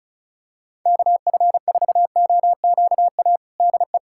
Morse code, heard as a keyed tone, about 34 wpm, 700 Hz. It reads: KF4OQA DE
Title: USA Callsign DE Callsign - 6 Characters - 35wpm